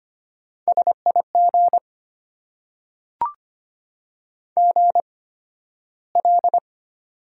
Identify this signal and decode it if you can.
Morse code, heard as a keyed tone, about 25 wpm, spaced as for 13 wpm, 700 Hz. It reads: SIG GL
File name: Top 90 QSO Elements - 8x Character Spacing - 25wpm